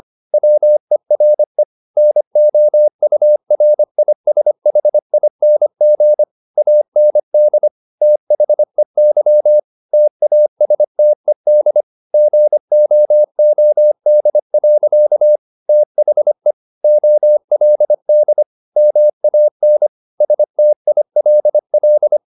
Morse code, heard as a keyed tone, about 25 wpm, 600 Hz. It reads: WERE NOURISHING AND THEY TASTED GOOD. THE OLD MAN STILL